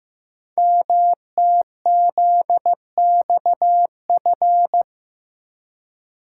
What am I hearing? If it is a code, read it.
Morse code, heard as a keyed tone, about 15 wpm, 700 Hz. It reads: MTZXF
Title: Code Groups - 5 Letters - 15wpm